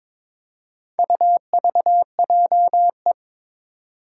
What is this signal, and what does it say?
Morse code, heard as a keyed tone, about 22 wpm, 700 Hz. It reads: UVJE
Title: Four Challenging Letter-Number Combinations 22wpm